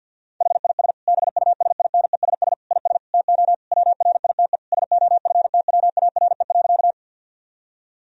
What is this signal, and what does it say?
Morse code, heard as a keyed tone, about 51 wpm, 700 Hz. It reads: HIS BUSINESS IS TO WRITE SOFTWARE.
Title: Sentences from Top 1000 Words - Review Part 3 - 50wpm